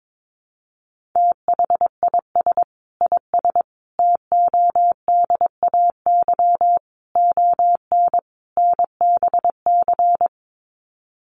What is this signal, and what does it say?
Morse code, heard as a keyed tone, about 22 wpm, 700 Hz. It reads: THIS IS TODAY ON NBC
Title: Sets of 5 Words Encore 22wpm